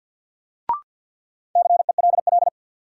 Morse code, heard as a keyed tone, about 51 wpm, 700 Hz. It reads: CELL
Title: Sentences from Top 1100 Words - New Word Drill Alt - 50wpm